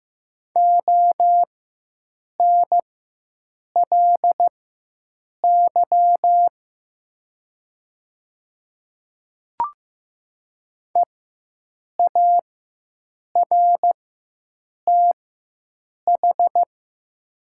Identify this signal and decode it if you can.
Morse code, heard as a keyed tone, about 15 wpm, spaced as for 7 wpm, 700 Hz. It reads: ONLY EARTH
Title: 4x Character Spacing - Top 200 Words - 15wpm